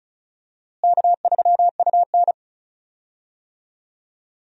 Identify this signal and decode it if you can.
Morse code, heard as a keyed tone, about 36 wpm, 700 Hz. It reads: K3UN
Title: USA Call Signs - 4 Characters - Repeated 2x - 35wpm